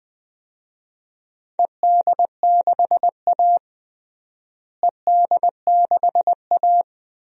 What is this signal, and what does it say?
Morse code, heard as a keyed tone, about 20 wpm, 700 Hz. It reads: ED6A ED6A
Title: DXCC-100 Call Signs - 4 Characters - Repeated 2x - 20wpm